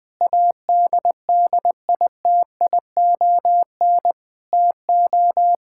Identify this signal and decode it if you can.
Morse code, heard as a keyed tone, about 21 wpm, 700 Hz. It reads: ADDITION TO
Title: Sentences from Top 1000 Words - Review Part 6 - 20wpm